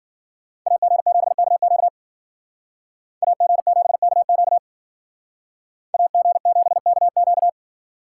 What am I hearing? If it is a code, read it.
Morse code, heard as a keyed tone, about 49 wpm, 700 Hz. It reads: AG7KQ AG7KQ AG7KQ
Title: Speed Racing - USA 2x2 Call Signs 40wpm